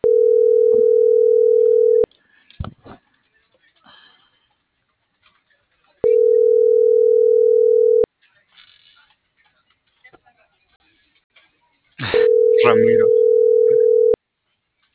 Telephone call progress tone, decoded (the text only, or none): ringback tone